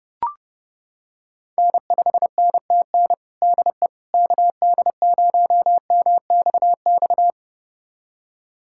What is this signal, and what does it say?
Morse code, heard as a keyed tone, about 30 wpm, 700 Hz. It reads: N5NTN DE KD0MXX
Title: USA Callsign DE Callsign 30wpm